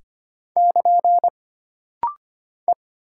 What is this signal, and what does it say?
Morse code, heard as a keyed tone, about 25 wpm, 700 Hz. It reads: <KN> E
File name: Single Character (Letter:Number:Punctuation:Prosign) - Rapid-Fire - 25wpm